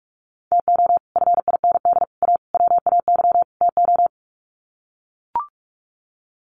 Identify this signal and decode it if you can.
Morse code, heard as a keyed tone, about 44 wpm, 700 Hz. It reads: TO FIND A WAY TO
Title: Sets of 5 Words 45wpm